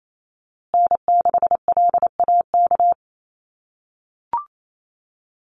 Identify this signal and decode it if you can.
Morse code, heard as a keyed tone, about 28 wpm, 700 Hz. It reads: N6LAK
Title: USA Call Signs Repeated 2x 28wpm